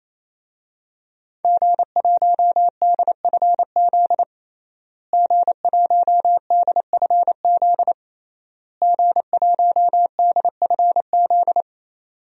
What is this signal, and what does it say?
Morse code, heard as a keyed tone, about 28 wpm, 700 Hz. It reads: G1DFZ G1DFZ G1DFZ
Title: DXCC-100 Call Signs - 5 Characters - Repeated 3x - 28wpm